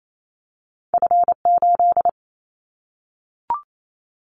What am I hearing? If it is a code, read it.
Morse code, heard as a keyed tone, about 28 wpm, 700 Hz. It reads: F8
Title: Callsign Prefixes for Top 25 DXCC - 28wpm